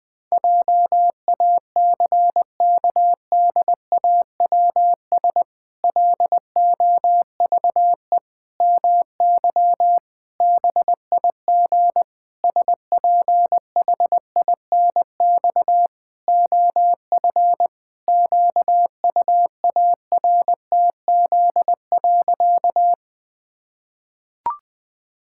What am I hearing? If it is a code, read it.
Morse code, heard as a keyed tone, about 20 wpm, 700 Hz. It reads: JACKDAWS LOVE MY BIG SPHINX OF QUARTZ.
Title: Pangram Challenge v2 - 20wpm